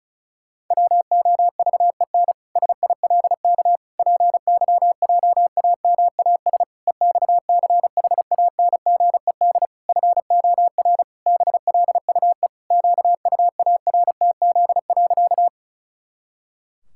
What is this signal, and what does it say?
Morse code, heard as a keyed tone, about 34 wpm, 700 Hz. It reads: WOVEN SILK PYJAMAS EXCHANGED FOR BLUE QUARTZ.
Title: Pangram Challenge v2 - 35wpm